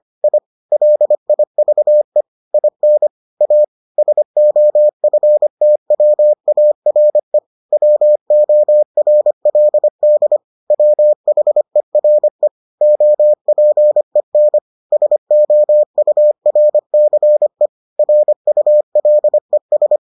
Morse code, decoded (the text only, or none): I LIVE IN A SOFTWARE WORLD WHERE OPEN SOURCE RULES